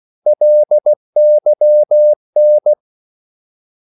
Morse code, heard as a keyed tone, about 16 wpm, 600 Hz. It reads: LYN
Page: Korte danske ord 16wpm